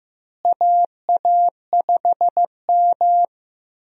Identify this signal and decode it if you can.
Morse code, heard as a keyed tone, about 15 wpm, 700 Hz. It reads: AA5M
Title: USA Call Signs - 4 Characters - Repeated 2x - 15wpm